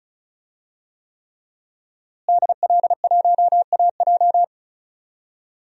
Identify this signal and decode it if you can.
Morse code, heard as a keyed tone, about 34 wpm, 700 Hz. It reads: DL1AJ
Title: DXCC-100 Call Signs - 5 Characters - Repeated 3x - 35wpm